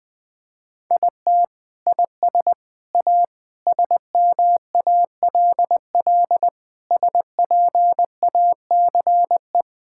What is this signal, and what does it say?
Morse code, heard as a keyed tone, about 20 wpm, 700 Hz. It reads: IT IS A SMALL SPACE
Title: Sentences from Top 1000 Words - Review Part 5 - 20wpm